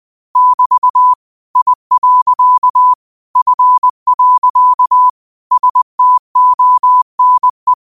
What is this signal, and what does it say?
Morse code, heard as a keyed tone, about 20 wpm, 1000 Hz. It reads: <BT> I. F. STONE